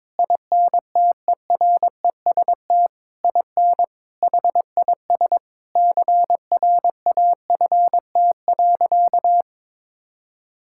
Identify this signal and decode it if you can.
Morse code, heard as a keyed tone, about 22 wpm, 700 Hz. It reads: INTEREST IN HIS CRAFT.
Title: Sentences from Top 1000 Words - Review Part 6 - 22wpm